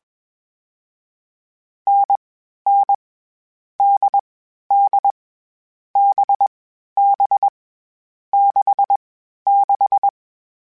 Morse code, heard as a keyed tone, about 21 wpm, 800 Hz. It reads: N N D D B B 6 6
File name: Short Runs / Short Runs-mp3s / NDB NDB6 @20